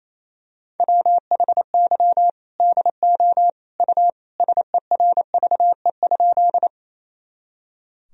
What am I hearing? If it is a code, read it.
Morse code, heard as a keyed tone, about 28 wpm, 700 Hz. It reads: WHY DO U SERVE?
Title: Sentences from Top 400 Words and Abbreviations 28wpm